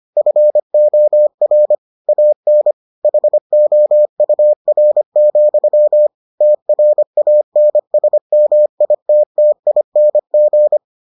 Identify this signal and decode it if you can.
Morse code, heard as a keyed tone, about 25 wpm, 600 Hz. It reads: FOR AN HOUR, TRANSMITTING